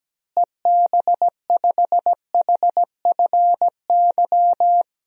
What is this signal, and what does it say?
Morse code, heard as a keyed tone, about 17 wpm, 700 Hz. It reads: EB5HFY
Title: DXCC-100 Call Signs - 6 Characters - Repeated 2x - 17wpm